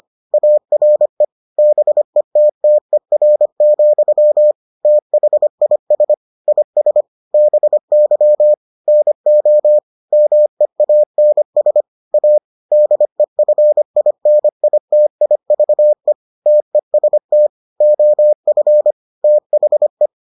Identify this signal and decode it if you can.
Morse code, heard as a keyed tone, about 25 wpm, 600 Hz. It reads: ARE BETTER, THIS IS BY NO MEANS A DEFINITIVE TEST OF THE